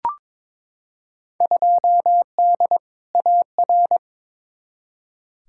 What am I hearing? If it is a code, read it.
Morse code, heard as a keyed tone, about 22 wpm, 700 Hz. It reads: 2D AR
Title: Field Day Exchanges Encore - 22wpm